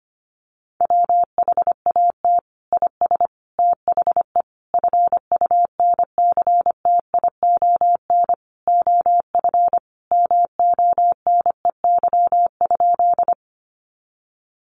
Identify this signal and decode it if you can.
Morse code, heard as a keyed tone, about 25 wpm, 700 Hz. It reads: WHAT IS THE FUNCTION OF MONEY?
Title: Sentences from Top 1100 Words 25wpm